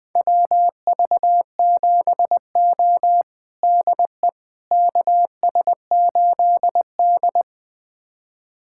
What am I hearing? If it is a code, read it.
Morse code, heard as a keyed tone, about 20 wpm, 700 Hz. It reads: WV7O DE KS8D
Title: USA Callsign DE Callsign - 4 Characters - 20wpm